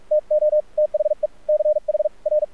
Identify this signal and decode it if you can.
Morse code, heard as a keyed tone, about 46 wpm, spaced as for 38 wpm, 610 Hz, with radio noise. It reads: TO THE CHR
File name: archived / CRSnet